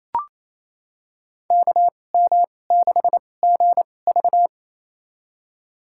Morse code, heard as a keyed tone, about 28 wpm, spaced as for 20 wpm, 700 Hz. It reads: KM6GV
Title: USA Call Signs - 2x Character Spacing - 28wpm